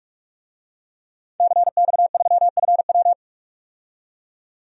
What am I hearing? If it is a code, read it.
Morse code, heard as a keyed tone, about 46 wpm, 700 Hz. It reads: CX3FW
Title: DXCC-100 Call Signs - 5 Characters - 45wpm